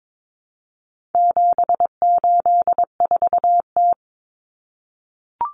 Morse code, heard as a keyed tone, about 22 wpm, 700 Hz. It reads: 784T
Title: Common Cut Numbers - 4-Digit 22wpm